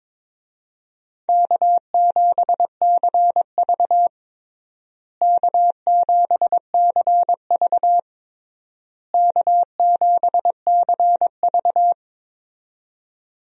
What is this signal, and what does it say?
Morse code, heard as a keyed tone, about 22 wpm, 700 Hz. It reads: K7CV K7CV K7CV
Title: USA Call Signs - 4 Characters - Repeated 3x - 22wpm